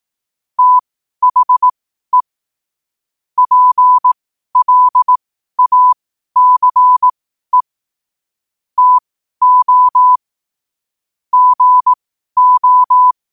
Morse code, heard as a keyed tone, about 18 wpm, spaced as for 12 wpm, 1000 Hz. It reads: THE PLACE TO GO
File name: qotdfiles / 12wpm